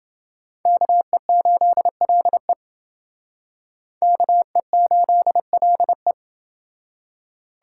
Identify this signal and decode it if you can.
Morse code, heard as a keyed tone, about 29 wpm, 700 Hz. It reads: KE8LE KE8LE